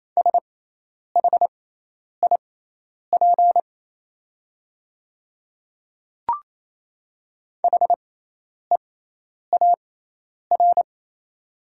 Morse code, heard as a keyed tone, about 27 wpm, spaced as for 10 wpm, 690 Hz. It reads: SHIP HEAR
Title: 6x Character Spacing - Top 500 Words - 28wpm